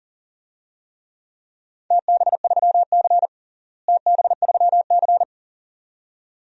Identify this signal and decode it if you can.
Morse code, heard as a keyed tone, about 40 wpm, 700 Hz. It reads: TB3C TB3C
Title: DXCC-100 Call Signs - 4 Characters - Repeated 2x - 40wpm